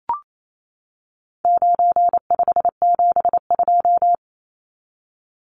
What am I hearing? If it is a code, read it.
Morse code, heard as a keyed tone, about 28 wpm, 700 Hz. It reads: 9572
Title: Numbers - 4-Digit 28wpm